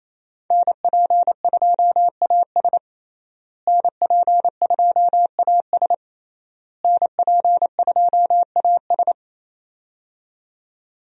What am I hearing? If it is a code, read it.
Morse code, heard as a keyed tone, about 28 wpm, 700 Hz. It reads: NP2AS NP2AS NP2AS
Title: USA Call Signs - 5 Characters - Repeated 3x - 28wpm